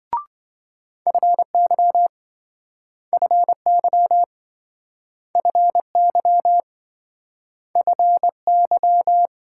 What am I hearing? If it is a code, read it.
Morse code, heard as a keyed tone, about 26 wpm, 700 Hz. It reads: FY FY FY FY